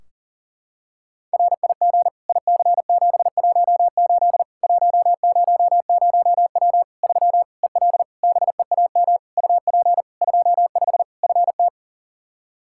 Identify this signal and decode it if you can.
Morse code, heard as a keyed tone, about 40 wpm, 700 Hz. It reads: RIG IC718 100W 3 EL BEAM UP 25 FT
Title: Practice QSO - Exchange 2 - Info - 40wpm